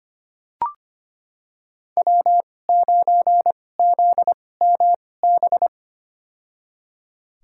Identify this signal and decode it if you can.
Morse code, heard as a keyed tone, about 25 wpm, spaced as for 18 wpm, 700 Hz. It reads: W9ZMB
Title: USA Call Signs - 5 Characters - 2x Character Spacing - 25wpm